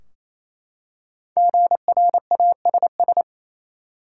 Morse code, heard as a keyed tone, about 28 wpm, 700 Hz. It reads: GRASS
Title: Five Letter Words - 28wpm